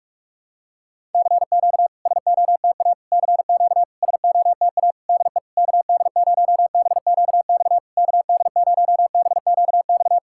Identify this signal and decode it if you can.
Morse code, heard as a keyed tone, about 46 wpm, 690 Hz. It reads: CQ SOTA CQ SOTA DE KD0BQX KD0BQX
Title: SOTA QSO Practice 45wpm